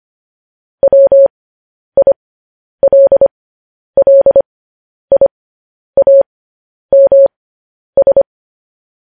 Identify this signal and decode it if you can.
Morse code, heard as a keyed tone, about 25 wpm, spaced as for 10 wpm, 550 Hz. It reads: WILLIAMS